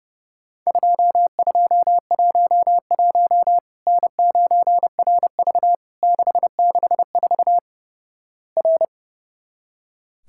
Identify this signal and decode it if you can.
Morse code, heard as a keyed tone, about 29 wpm, 700 Hz. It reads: 2211 N9RV 664 R
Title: Worked All Europe DX Contest - QTC Exchange - 30wpm